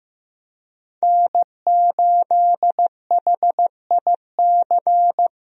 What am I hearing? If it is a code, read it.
Morse code, heard as a keyed tone, about 16 wpm, 700 Hz. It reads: N8HIC
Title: USA Call Signs - 5 Characters - 15wpm